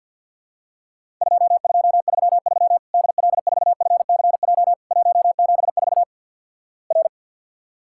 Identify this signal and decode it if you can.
Morse code, heard as a keyed tone, about 50 wpm, 700 Hz. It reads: 2233 DL4FCJ 174 R
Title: Worked All Europe DX Contest - QTC Exchange - 50wpm